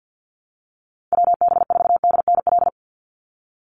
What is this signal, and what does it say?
Morse code, heard as a keyed tone, about 50 wpm, 700 Hz. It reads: WB4DNL
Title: Call Signs Repeated 2x - 50wpm